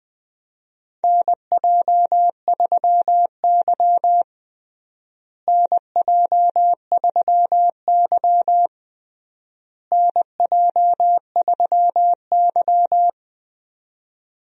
Morse code, heard as a keyed tone, about 21 wpm, 700 Hz. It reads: NJ3Y NJ3Y NJ3Y
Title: USA Call Signs - 4 Characters - Repeated 3x - 20wpm